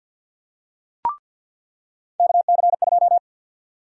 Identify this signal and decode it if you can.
Morse code, heard as a keyed tone, about 50 wpm, 700 Hz. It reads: KC2
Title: USA Call Sign Prefixes - 50wpm